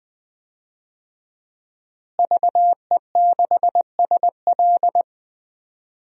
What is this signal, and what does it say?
Morse code, heard as a keyed tone, about 20 wpm, 700 Hz. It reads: VE6SL
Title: DXCC-100 Call Signs - 5 Characters - Repeated 2x - 20wpm